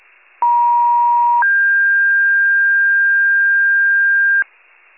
Two-tone page, heard as a one-second tone, then a three-second tone, about 950 then 1600 Hz, with noise. It is heard from 0.4 to 4.4 s.
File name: wcalert